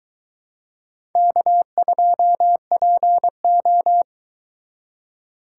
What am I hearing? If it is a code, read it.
Morse code, heard as a keyed tone, about 23 wpm, 700 Hz. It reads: K2PO